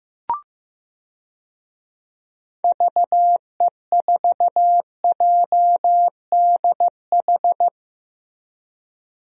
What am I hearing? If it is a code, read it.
Morse code, heard as a keyed tone, about 15 wpm, 700 Hz. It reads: VE4JDH
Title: DXCC-100 Call Signs - 6 Characters - Repeated 2x - 15wpm